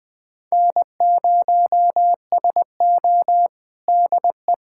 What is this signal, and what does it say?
Morse code, heard as a keyed tone, about 21 wpm, 700 Hz. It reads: N0SO DE
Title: USA Callsign DE Callsign - 4 Characters - 20wpm